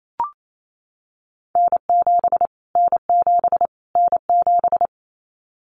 Morse code, heard as a keyed tone, about 28 wpm, 700 Hz. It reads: N7 N7 N7
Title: Callsign Prefixes for Top 25 DXCC - Repeated 3x - 28wpm